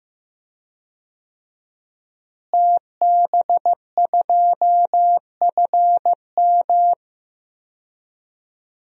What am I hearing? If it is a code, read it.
Morse code, heard as a keyed tone, about 15 wpm, 700 Hz. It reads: TB2FM
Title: DXCC-100 Call Signs - 5 Characters - Repeated 2x - 15wpm